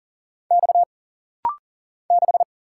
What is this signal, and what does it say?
Morse code, heard as a keyed tone, about 40 wpm, 700 Hz. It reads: X 6
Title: Single Letter-Number - Rapid-Fire 40wpm